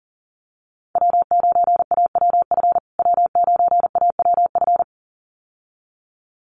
Morse code, heard as a keyed tone, about 40 wpm, 700 Hz. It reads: W9AWF W9AWF
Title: USA Call Signs Repeated 2x 40wpm